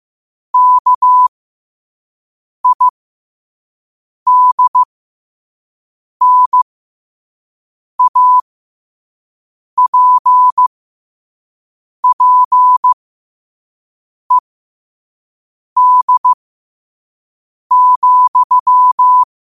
Morse code, heard as a keyed tone, about 15 wpm, spaced as for 5 wpm, 1000 Hz. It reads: KIDNAPPED,
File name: qotdfiles / 5wpm